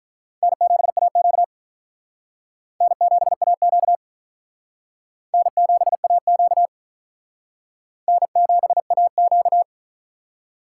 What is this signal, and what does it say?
Morse code, heard as a keyed tone, about 42 wpm, 700 Hz. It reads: N7AQ N7AQ N7AQ N7AQ
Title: Speed Racing - USA 1x2 Call Signs 35wpm